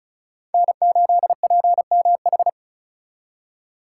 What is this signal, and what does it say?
Morse code, heard as a keyed tone, about 44 wpm, 700 Hz. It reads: N8PMH
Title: Speed Racing - USA 1x3 Call Signs 35wpm